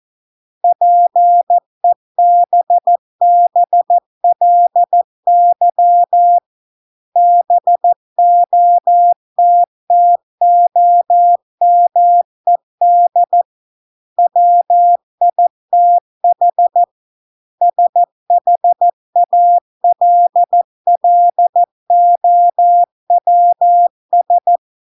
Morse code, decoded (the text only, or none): PEBBLY BOTTOMED WITH SHALLOWS